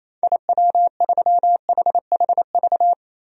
Morse code, heard as a keyed tone, about 28 wpm, 700 Hz. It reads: IW3HHV
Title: DXCC-100 Call Signs - 6 Characters - 28wpm